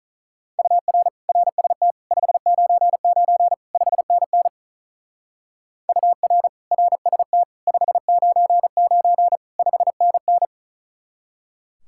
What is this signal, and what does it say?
Morse code, heard as a keyed tone, about 42 wpm, 700 Hz. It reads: UR RST 599 5NN UR RST 599 5NN
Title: Speed-Racing Common QSO Phrases - 35wpm